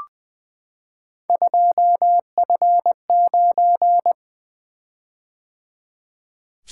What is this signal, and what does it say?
Morse code, heard as a keyed tone, about 20 wpm, 700 Hz. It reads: 2F9
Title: Three Letter-Number Combinations - 20wpm